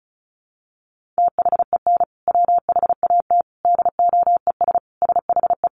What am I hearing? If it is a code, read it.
Morse code, heard as a keyed tone, about 34 wpm, 700 Hz. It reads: THEN WHAT DOES SHE
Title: Sentences from Top 1500 Words 35wpm